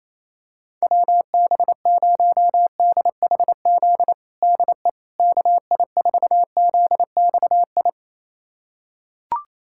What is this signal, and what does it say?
Morse code, heard as a keyed tone, about 28 wpm, 700 Hz. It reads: WB0DHZ DE KI4ZXI
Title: USA Callsign DE Callsign - 6 Characters - 28wpm